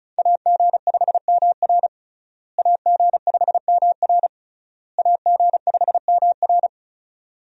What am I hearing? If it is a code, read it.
Morse code, heard as a keyed tone, about 36 wpm, 700 Hz. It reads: AG5MR AG5MR AG5MR
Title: USA Call Signs - 5 Characters - Repeated 3x - 35wpm